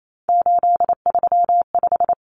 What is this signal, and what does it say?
Morse code, heard as a keyed tone, about 28 wpm, 700 Hz. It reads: 835
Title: Numbers - 3-Digit 28wpm